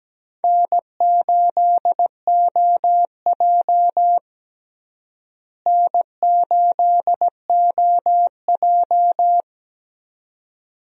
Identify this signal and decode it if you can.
Morse code, heard as a keyed tone, about 17 wpm, 700 Hz. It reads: N8OJ N8OJ
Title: USA Call Signs - 4 Characters - Repeated 2x - 17wpm